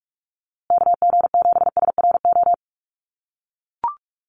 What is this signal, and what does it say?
Morse code, heard as a keyed tone, about 44 wpm, 700 Hz. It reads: KG7SRO
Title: USA Call Signs Repeated 3x 45wpm